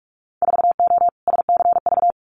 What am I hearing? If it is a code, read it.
Morse code, heard as a keyed tone, about 46 wpm, 700 Hz. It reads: 4O SCV
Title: Winter Field Day Exchanges 45wpm